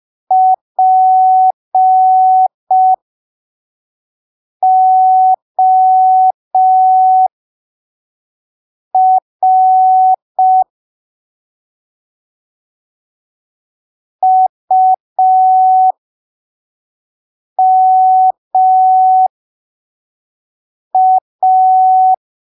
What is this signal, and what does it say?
Morse code, heard as a keyed tone, about 5 wpm, spaced as for 3 wpm, 750 Hz. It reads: POR UMA
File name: arquivos / texto / texto 11 05ppm